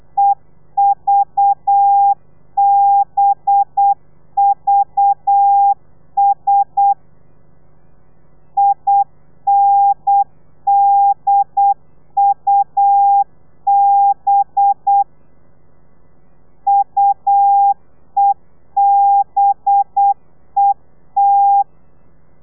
Morse code, heard as a keyed tone, about 8 wpm, 800 Hz. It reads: EVBVS INDUB UEBET
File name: telegrafia / i2tfj / Lez4 35